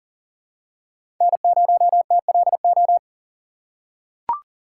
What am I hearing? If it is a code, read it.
Morse code, heard as a keyed tone, about 40 wpm, 700 Hz. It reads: N0TLO
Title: USA Call Signs - 5 Characters - 4x Character Spacing - 40wpm